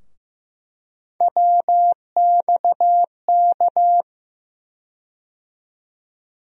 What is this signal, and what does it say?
Morse code, heard as a keyed tone, about 15 wpm, 700 Hz. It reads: WXK
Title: USA Call Sign Postfixes - 3 Letters 15wpm